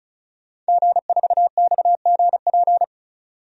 Morse code, heard as a keyed tone, about 34 wpm, 700 Hz. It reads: G4XGP
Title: DXCC-100 Call Signs - 5 Characters - 35wpm